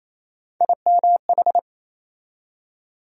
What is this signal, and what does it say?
Morse code, heard as a keyed tone, about 28 wpm, 700 Hz. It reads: IMH
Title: USA Call Sign Postfixes - 3 Letters 28wpm